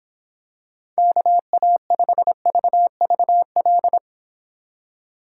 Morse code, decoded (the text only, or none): KA5VVL